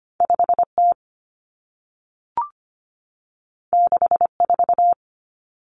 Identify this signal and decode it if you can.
Morse code, heard as a keyed tone, about 25 wpm, 700 Hz. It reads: 5T 64
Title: Common Cut Numbers - 2-Digit 25wpm